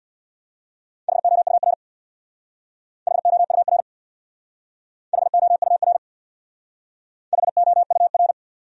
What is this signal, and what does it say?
Morse code, heard as a keyed tone, about 65 wpm, 700 Hz. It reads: HOUR HOUR HOUR HOUR